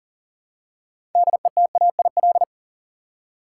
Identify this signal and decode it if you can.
Morse code, heard as a keyed tone, about 40 wpm, 700 Hz. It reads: DETAIL
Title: Six Letter Words - 40wpm